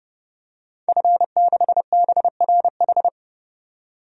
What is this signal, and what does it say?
Morse code, heard as a keyed tone, about 32 wpm, 700 Hz. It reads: F6BRH
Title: DXCC-100 Call Signs - 5 Characters - 30wpm